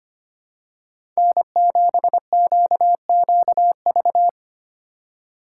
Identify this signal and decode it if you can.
Morse code, heard as a keyed tone, about 25 wpm, 700 Hz. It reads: N7QQV
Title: USA Call Signs - 5 Characters - 8x Character Spacing - 25wpm